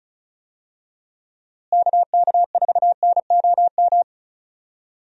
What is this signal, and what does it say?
Morse code, heard as a keyed tone, about 34 wpm, 700 Hz. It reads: KK4NOM
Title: USA Call Signs - 6 Characters - Repeated 3x - 35wpm